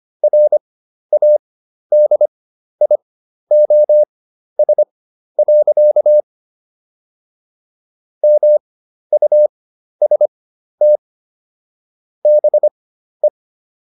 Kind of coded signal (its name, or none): Morse code